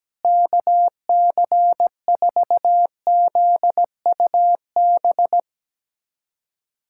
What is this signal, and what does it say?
Morse code, heard as a keyed tone, about 17 wpm, 700 Hz. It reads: KC4ZUB
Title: USA Callsign DE Callsign - 6 Characters - 17wpm